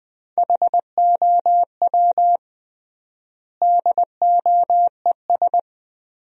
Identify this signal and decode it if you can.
Morse code, heard as a keyed tone, about 20 wpm, 700 Hz. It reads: HOW DOES
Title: 3x Word-Spacing - Sentences from Top 400 Words - 20wpm